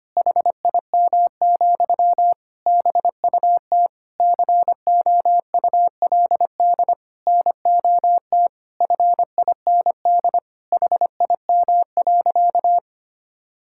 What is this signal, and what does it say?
Morse code, heard as a keyed tone, about 25 wpm, 700 Hz. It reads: HIM, BUT COULD NOT FIND HIM.
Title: Sentences from Top 1000 Words - Review Part 5 - 25wpm